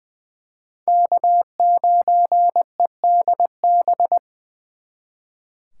Morse code, heard as a keyed tone, about 23 wpm, 700 Hz. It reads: K9EDB
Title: Speed Racing - USA 1x3 Call Signs 20wpm